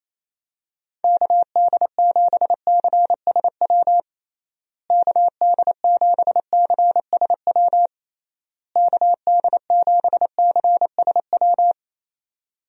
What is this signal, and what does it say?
Morse code, heard as a keyed tone, about 28 wpm, 700 Hz. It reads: KD7CSW KD7CSW KD7CSW
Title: USA Call Signs - 6 Characters - Repeated 3x - 28wpm